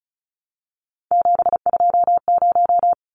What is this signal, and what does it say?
Morse code, heard as a keyed tone, about 34 wpm, 700 Hz. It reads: 720
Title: Numbers - 3-Digit 35wpm